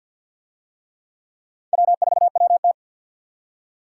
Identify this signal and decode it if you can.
Morse code, heard as a keyed tone, about 49 wpm, 700 Hz. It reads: W4WT
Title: USA Call Signs - 4 Characters - 50wpm